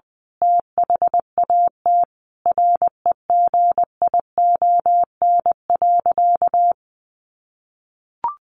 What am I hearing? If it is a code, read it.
Morse code, heard as a keyed tone, about 19 wpm, 700 Hz. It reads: THAT REGION.
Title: A Sound Of Thunder 20wpm